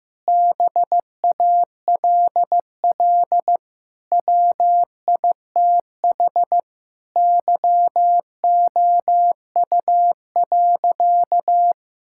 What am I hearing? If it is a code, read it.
Morse code, heard as a keyed tone, about 15 wpm, 700 Hz. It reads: BALL WITH YOU.
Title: Sentences from Top 1000 Words - Review Part 6 - 15wpm